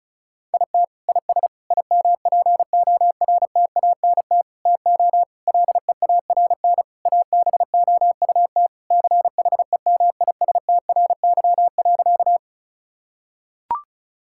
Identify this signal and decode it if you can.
Morse code, heard as a keyed tone, about 36 wpm, 700 Hz. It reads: IT IS IMPORTANT TO LEARN ABOUT CHEMISTRY.
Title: Sentences from Top 1000 Words - Review Part 6 - 35wpm